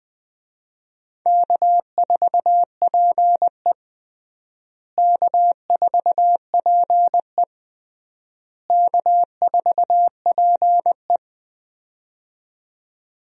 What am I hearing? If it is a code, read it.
Morse code, heard as a keyed tone, about 20 wpm, 700 Hz. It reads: K4PE K4PE K4PE
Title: USA Call Signs - 4 Characters - Repeated 3x - 20wpm